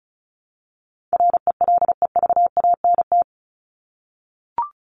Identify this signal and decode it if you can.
Morse code, heard as a keyed tone, about 34 wpm, 700 Hz. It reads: RELEVANT
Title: Top 1001-1250 Words 35wpm